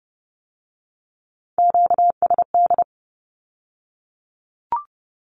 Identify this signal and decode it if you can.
Morse code, heard as a keyed tone, about 30 wpm, 700 Hz. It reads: QSD
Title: Q-Codes with Spoken Meaning - 30wpm